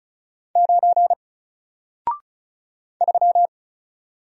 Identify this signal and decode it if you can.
Morse code, heard as a keyed tone, about 36 wpm, 700 Hz. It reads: 9 3
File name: Single-Letter-Number-35wpm